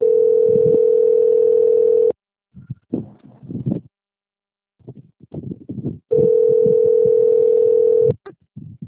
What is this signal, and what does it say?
Ringback tone (North America), 440 + 480 Hz.